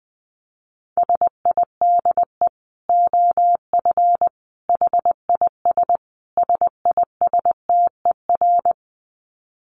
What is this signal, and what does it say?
Morse code, heard as a keyed tone, about 20 wpm, 700 Hz. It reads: SIDE OF HIS SISTER
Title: Dit Challenge 20wpm